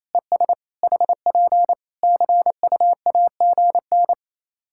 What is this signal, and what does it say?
Morse code, heard as a keyed tone, about 27 wpm, 700 Hz. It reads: ES HP CUAGN
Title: K1USN SST QSO Practice 28wpm